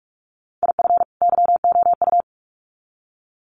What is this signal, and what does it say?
Morse code, heard as a keyed tone, about 44 wpm, 700 Hz. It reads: IF YOU
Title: Sets of 2 Words 45wpm